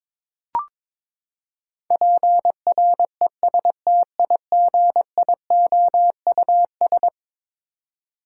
Morse code, heard as a keyed tone, about 22 wpm, 700 Hz. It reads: PRESTIGIOUS
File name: practice / 22wpm / Eleven Letter Words - 22wpm